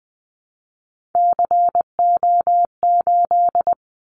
Morse code, heard as a keyed tone, about 20 wpm, 700 Hz. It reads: CO8
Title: Callsign Prefixes for Top 100 DXCC - 20wpm